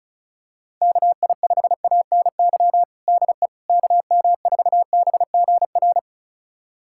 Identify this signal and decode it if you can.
Morse code, heard as a keyed tone, about 36 wpm, 700 Hz. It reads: KI5ANY DE KM4BGR
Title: USA Callsign DE Callsign 35wpm